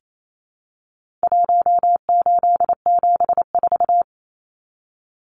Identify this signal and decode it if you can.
Morse code, heard as a keyed tone, about 28 wpm, 700 Hz. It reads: 1874
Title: Numbers - Contest Style - 4-Digit 28wpm